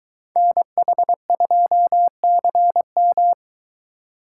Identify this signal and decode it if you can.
Morse code, heard as a keyed tone, about 23 wpm, 700 Hz. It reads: NH2CM